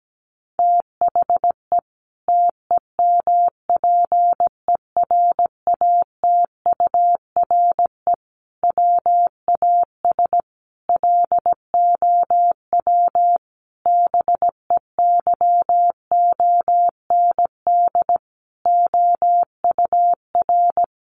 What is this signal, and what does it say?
Morse code, heard as a keyed tone, about 17 wpm, 700 Hz. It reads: THE TEMPERATURE WAS LOW BEYOND OUR
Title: Sentences from Top 1500 Words 17wpm